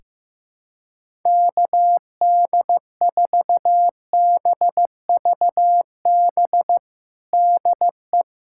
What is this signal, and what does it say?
Morse code, heard as a keyed tone, about 15 wpm, 700 Hz. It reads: KD4BVB DE
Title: USA Callsign DE Callsign - 6 Characters - 15wpm